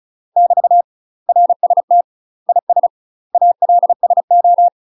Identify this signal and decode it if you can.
Morse code, heard as a keyed tone, about 35 wpm, 700 Hz. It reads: <BT> RST IS ALSO
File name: UK MASTER QSOs TRAINING / 35WPM / UK MASTER QSOs 0003